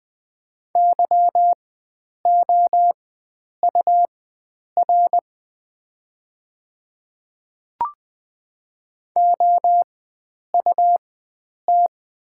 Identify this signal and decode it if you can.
Morse code, heard as a keyed tone, about 20 wpm, spaced as for 9 wpm, 700 Hz. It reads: YOUR OUT
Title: 4x Character Spacing - Top 100 Words - 20wpm